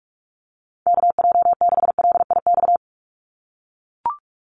Morse code, heard as a keyed tone, about 44 wpm, 700 Hz. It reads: KJ6LIX
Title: USA Call Signs 45wpm